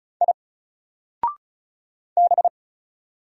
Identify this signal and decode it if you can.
Morse code, heard as a keyed tone, about 36 wpm, 700 Hz. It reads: I B
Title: Single Letter-Number Alt 35wpm